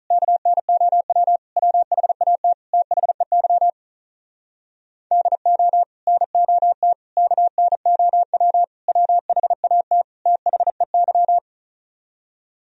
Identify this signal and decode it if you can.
Morse code, heard as a keyed tone, about 47 wpm, 700 Hz. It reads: KNOW WHAT THEY DO NOT KNOW WHAT THEY